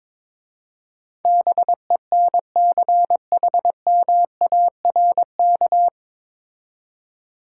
Morse code, heard as a keyed tone, about 22 wpm, 690 Hz. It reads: BENCHMARK
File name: 9 Letter Words - First 3 Letters Given - 22wpm